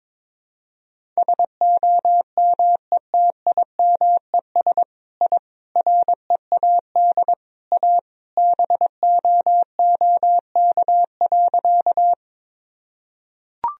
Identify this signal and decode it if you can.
Morse code, heard as a keyed tone, about 22 wpm, 700 Hz. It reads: SOMETIMES I READ A BOOK.
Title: Sentences from Top 1000 Words - Review Part 3 - 22wpm